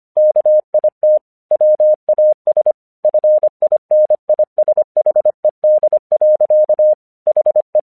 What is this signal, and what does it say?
Morse code, heard as a keyed tone, about 25 wpm, 610 Hz. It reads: KIT WAS FINISHED. HE
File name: 1 English 25wpm / AQC27